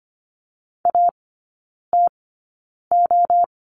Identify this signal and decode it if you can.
Morse code, heard as a keyed tone, about 25 wpm, spaced as for 9 wpm, 700 Hz. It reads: ATO
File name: course / 25:10wpm / 100 Lesson 17e - Letter u - Three Characters 25wpm